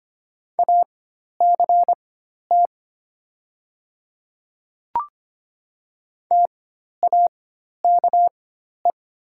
Morse code, heard as a keyed tone, about 25 wpm, spaced as for 12 wpm, 700 Hz. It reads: ACT TAKE
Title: 4x Character Spacing - Top 200 Words - 25wpm